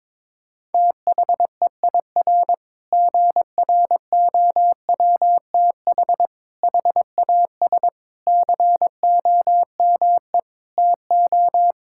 Morse code, heard as a keyed tone, about 22 wpm, 700 Hz. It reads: THEIR GROWTH HAS COME TO
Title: Sentences from Top 1000 Words - Review Part 5 - 22wpm